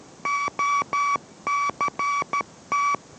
Morse code, heard as a keyed tone, about 13 wpm, 1140 Hz. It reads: OCT